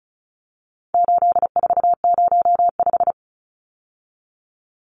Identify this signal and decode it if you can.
Morse code, heard as a keyed tone, about 34 wpm, 700 Hz. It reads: 8405
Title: Numbers - 4-Digit 35wpm